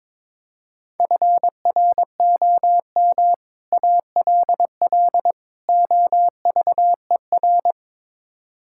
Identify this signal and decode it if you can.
Morse code, heard as a keyed tone, about 22 wpm, 700 Hz. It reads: FROM ALL OVER
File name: practice / 22wpm / Build A Phrase - 5-Word Phrases - 22wpm